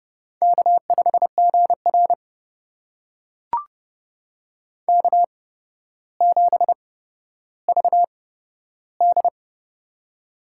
Morse code, heard as a keyed tone, about 30 wpm, spaced as for 13 wpm, 700 Hz. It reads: K5GR K7VD
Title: USA Call Signs - 4 Characters - 8x Character Spacing - 30wpm